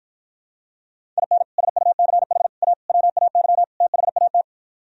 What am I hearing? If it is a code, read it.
Morse code, heard as a keyed tone, about 52 wpm, 700 Hz. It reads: IN SUCH A WAY THAT